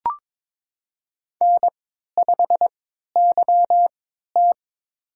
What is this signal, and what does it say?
Morse code, heard as a keyed tone, about 22 wpm, spaced as for 12 wpm, 700 Hz. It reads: N5YT
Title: USA Call Signs - 3x Character Spacing - 22wpm